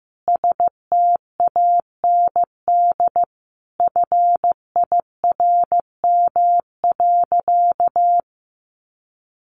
Morse code, heard as a keyed tone, about 15 wpm, 700 Hz. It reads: STAND FIRM.
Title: Sentences from Top 1400 Words 15wpm